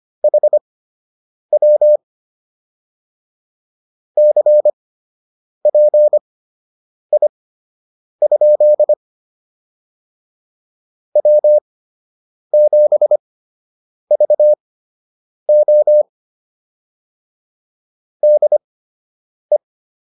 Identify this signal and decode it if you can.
Morse code, heard as a keyed tone, about 25 wpm, spaced as for 8 wpm, 600 Hz. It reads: HW CPI? W7VO DE